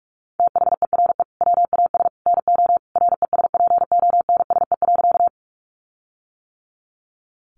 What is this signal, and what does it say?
Morse code, heard as a keyed tone, about 44 wpm, 700 Hz. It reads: THERE WAS NO RESPONSE.
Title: Sentences from Top 1400 Words 45wpm